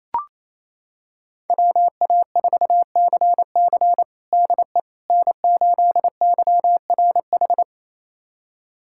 Morse code, heard as a keyed tone, about 28 wpm, 700 Hz. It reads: WA4CC DE N8YRH
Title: USA Callsign DE Callsign - 5 Characters - 28wpm